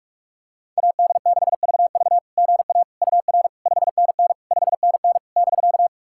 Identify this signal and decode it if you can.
Morse code, heard as a keyed tone, about 46 wpm, 690 Hz. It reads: AD6VV GA UR 5NN 5NN <BK>
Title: SOTA QSO Practice 45wpm